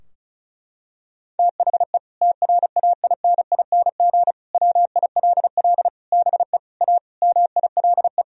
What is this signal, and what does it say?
Morse code, heard as a keyed tone, about 36 wpm, 700 Hz. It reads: THE TRAINING WILL BE A MILE
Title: Sentences from Top 1000 Words - Review Part 4 - 35wpm